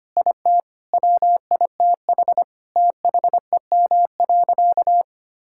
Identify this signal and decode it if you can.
Morse code, heard as a keyed tone, about 25 wpm, 700 Hz. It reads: IT WITH THEM.
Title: Sentences from Top 1000 Words - Review Part 6 - 25wpm